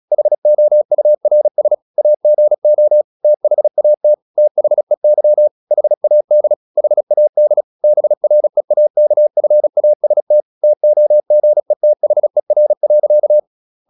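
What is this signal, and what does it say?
Morse code, heard as a keyed tone, about 37 wpm, 600 Hz. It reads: HOURS AGO THAT THEY HAD HAD BREAKFAST TOGETHER.
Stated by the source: Morse practice at 36 wpm, text of Ken Follett's novel Never, from CW med Gnister.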